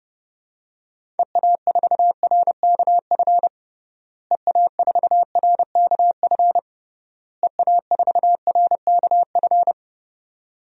Morse code, heard as a keyed tone, about 30 wpm, 700 Hz. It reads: EA4RKF EA4RKF EA4RKF
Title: DXCC-100 Call Signs - 6 Characters - Repeated 3x - 30wpm